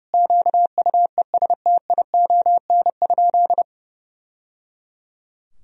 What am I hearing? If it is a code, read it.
Morse code, heard as a keyed tone, about 30 wpm, 700 Hz. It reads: QUESTION?
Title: Sentences from Top 1000 Words - Review Part 4 - 30wpm